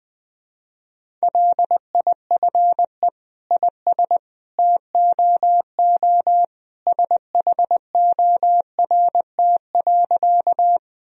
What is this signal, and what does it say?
Morse code, heard as a keyed tone, about 20 wpm, 700 Hz. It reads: LIFE IS TOO SHORT.
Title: Sentences from Top 1000 Words - Review Part 2 - 20wpm